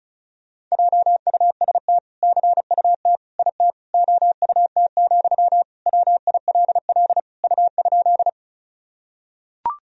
Morse code, heard as a keyed tone, about 36 wpm, 700 Hz. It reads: JUST CUT IT OUT, WILL U?
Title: Sentences from Top 300 Words and Abbreviations 35wpm